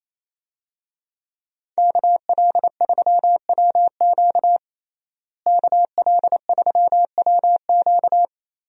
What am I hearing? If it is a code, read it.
Morse code, heard as a keyed tone, about 28 wpm, 700 Hz. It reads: KL3WQ KL3WQ
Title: USA Call Signs - 5 Characters - Repeated 3x - 28wpm